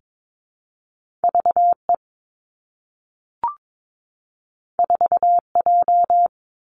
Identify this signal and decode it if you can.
Morse code, heard as a keyed tone, about 22 wpm, 700 Hz. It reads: V E 4 J
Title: Two Challenging Letter-Number Combinations 22wpm